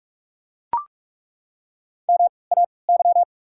Morse code, heard as a keyed tone, about 43 wpm, spaced as for 25 wpm, 690 Hz. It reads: MAY
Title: 3x Character Spacing - Top 200 Words - 45wpm